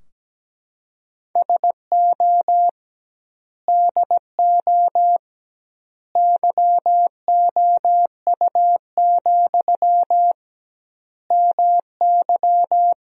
Morse code, heard as a keyed tone, about 17 wpm, 700 Hz. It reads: SO DO YOU, MY
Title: 2x Word-Spacing - Sentences from Top 400 Words - 17wpm